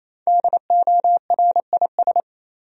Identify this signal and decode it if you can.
Morse code, heard as a keyed tone, about 28 wpm, 700 Hz. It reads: DORIS
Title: USA Common Names - Max 5 Letters - 28wpm